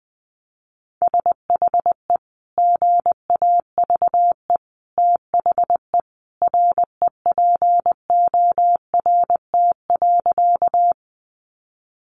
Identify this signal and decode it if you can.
Morse code, heard as a keyed tone, about 19 wpm, 700 Hz. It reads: SHE GAVE THE REPORT.
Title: Sentences from Top 900 Words 20wpm